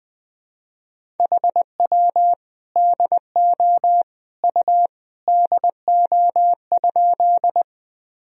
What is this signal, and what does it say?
Morse code, heard as a keyed tone, about 20 wpm, 700 Hz. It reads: HW DO U DO?
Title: Sentences from Top 100 Words and Abbreviations 20wpm